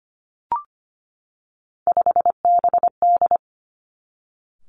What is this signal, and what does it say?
Morse code, heard as a keyed tone, about 25 wpm, 700 Hz. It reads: 5BD
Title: Three Challenging Letter-Number Combinations 25wpm